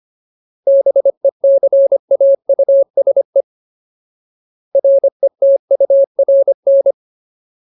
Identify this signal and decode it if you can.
Morse code, heard as a keyed tone, about 25 wpm, 550 Hz. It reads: BECAUSE RETURN
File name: Ultimate Word List / 25WPM Ultimate